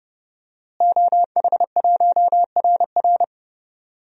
Morse code, heard as a keyed tone, about 30 wpm, 700 Hz. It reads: OH1RR
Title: DXCC-100 Call Signs - 5 Characters - Repeated 3x - 30wpm